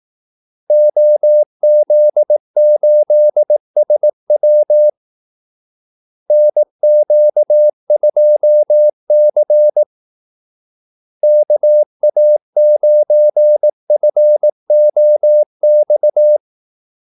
Morse code, heard as a keyed tone, about 18 wpm, 600 Hz. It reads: OZ8SW NQ2C KA9FOX
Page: Kaldesignaler 18 wpm